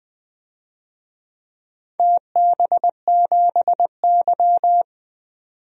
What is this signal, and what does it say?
Morse code, heard as a keyed tone, about 20 wpm, 700 Hz. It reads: TB7Y
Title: DXCC-100 Call Signs - 4 Characters - 20wpm